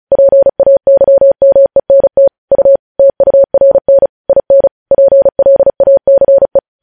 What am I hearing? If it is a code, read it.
Morse code, heard as a keyed tone, about 36 wpm, 550 Hz. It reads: PAYMENT U TURN IN PLACE
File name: podcast / News headlines-1x-35x35